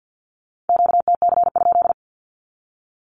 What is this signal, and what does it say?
Morse code, heard as a keyed tone, about 50 wpm, 700 Hz. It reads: QTC?
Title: Q-Codes with Spoken Meaning - 50wpm